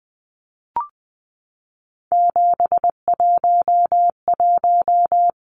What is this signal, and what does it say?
Morse code, heard as a keyed tone, about 20 wpm, 700 Hz. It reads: 711
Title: Numbers - 3-Digit 20wpm